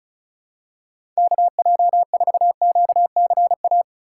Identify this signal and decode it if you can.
Morse code, heard as a keyed tone, about 36 wpm, 700 Hz. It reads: KJ4QCA
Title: USA Call Signs - 6 Characters - 3x Character Spacing - 35wpm